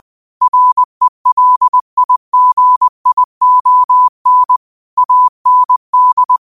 Morse code, heard as a keyed tone, about 20 wpm, 1000 Hz. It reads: RELIGION AND